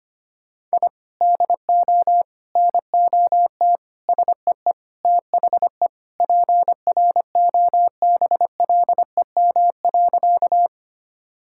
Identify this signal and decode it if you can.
Morse code, heard as a keyed tone, about 25 wpm, 700 Hz. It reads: I DO NOT SEE THE PROBLEM.
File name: practice / 25wpm / Sentences with 4-Word Phrases - 25wpm